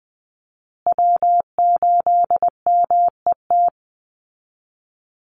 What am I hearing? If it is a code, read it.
Morse code, heard as a keyed tone, about 20 wpm, 700 Hz. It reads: W8MET
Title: Call Signs Repeated 2x - 20wpm